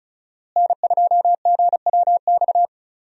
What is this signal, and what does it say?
Morse code, heard as a keyed tone, about 36 wpm, 700 Hz. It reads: N2GWX
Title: USA Call Signs - 5 Characters - Repeated 2x - 35wpm